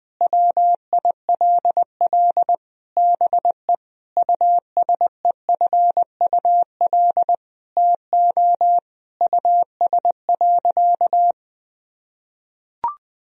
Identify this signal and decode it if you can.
Morse code, heard as a keyed tone, about 20 wpm, 700 Hz. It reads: WILL BE USEFUL TO US.
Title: Sentences from Top 1000 Words - Review Part 4 - 20wpm